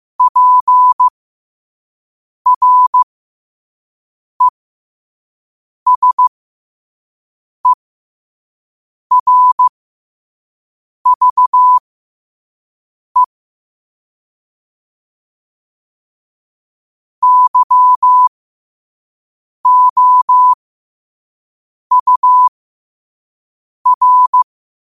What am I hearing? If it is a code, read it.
Morse code, heard as a keyed tone, about 15 wpm, spaced as for 5 wpm, 1000 Hz. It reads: PRESERVE YOUR
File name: qotdfiles / 5wpm